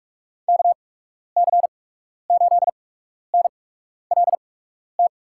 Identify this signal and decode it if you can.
Morse code, heard as a keyed tone, about 47 wpm, spaced as for 12 wpm, 700 Hz. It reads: KC8NLT
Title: USA Call Signs - 6 Characters - 8x Character Spacing - 45wpm